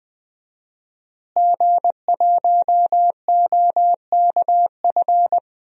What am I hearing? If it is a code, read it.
Morse code, heard as a keyed tone, about 20 wpm, 700 Hz. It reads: G1OKF
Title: DXCC-100 Call Signs - 5 Characters - Repeated 2x - 20wpm